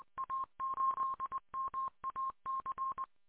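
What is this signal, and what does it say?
Morse code, heard as a keyed tone, about 20 wpm, 1060 Hz. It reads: A8MAC
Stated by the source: Recorded from WA8MAC on 147.200 MHz.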